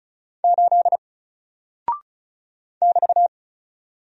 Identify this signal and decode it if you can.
Morse code, heard as a keyed tone, about 36 wpm, 700 Hz. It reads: 8 <BT>
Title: Single Character (Letter:Number:Punctuation:Prosign) Alt 35wpm